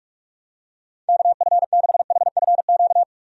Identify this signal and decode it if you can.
Morse code, heard as a keyed tone, about 46 wpm, 700 Hz. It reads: KF6HFQ
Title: USA Call Signs - 6 Characters - 45wpm